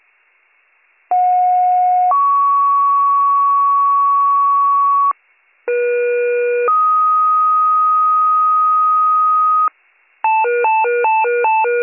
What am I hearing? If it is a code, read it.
Two-tone page, heard as a one-second tone, then a three-second tone, about 730 then 1120 Hz, about 480 then 1250 Hz. It is heard from 1.1 to 9.7 s.